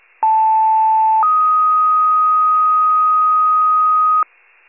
Two-tone page, heard as a one-second tone, then a three-second tone, about 880 then 1250 Hz. It is heard from 0.2 to 4.2 s.